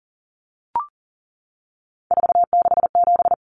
Morse code, heard as a keyed tone, about 40 wpm, 700 Hz. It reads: 467
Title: Numbers - 3-Digit 40wpm